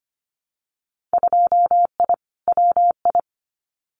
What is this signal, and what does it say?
Morse code, heard as a keyed tone, about 25 wpm, 700 Hz. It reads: 2I WI
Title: Winter Field Day Exchanges 25wpm